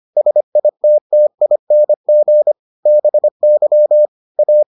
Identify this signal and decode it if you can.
Morse code, heard as a keyed tone, about 25 wpm, 600 Hz. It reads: SITTING BY A